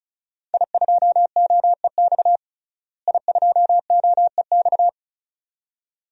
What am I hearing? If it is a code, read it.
Morse code, heard as a keyed tone, about 34 wpm, 700 Hz. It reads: I2OEX I2OEX
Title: DXCC-100 Call Signs - 5 Characters - Repeated 2x - 35wpm